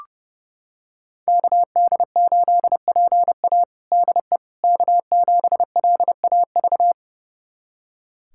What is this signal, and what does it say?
Morse code, heard as a keyed tone, about 32 wpm, 700 Hz. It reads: KD8PA DE K7LAV
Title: USA Callsign DE Callsign - 5 Characters - 30wpm